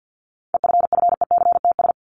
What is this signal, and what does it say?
Morse code, heard as a keyed tone, about 48 wpm, 700 Hz. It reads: EFFECTS
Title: ES:S Suffix - Top 100 Words with Base Words 50wpm